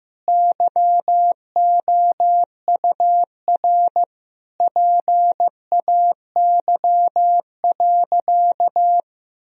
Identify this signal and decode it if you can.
Morse code, heard as a keyed tone, about 16 wpm, 700 Hz. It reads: YOUR PAY.
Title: Sentences from Top 1000 Words - Review Part 4 - 15wpm